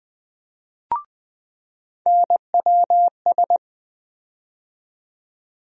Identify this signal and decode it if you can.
Morse code, heard as a keyed tone, about 21 wpm, 700 Hz. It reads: NWS
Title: USA Call Sign Postfixes - 3 Letters 20wpm